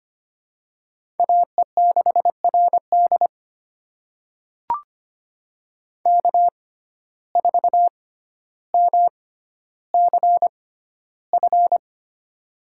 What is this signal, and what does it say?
Morse code, heard as a keyed tone, about 25 wpm, spaced as for 13 wpm, 700 Hz. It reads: AE6RD K4MCF
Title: USA Call Signs - 5 Characters - 6x Character Spacing - 25wpm